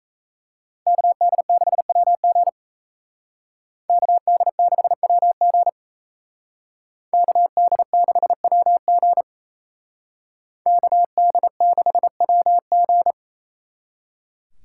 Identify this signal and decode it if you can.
Morse code, heard as a keyed tone, about 36 wpm, 700 Hz. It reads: KD6WG KD6WG KD6WG KD6WG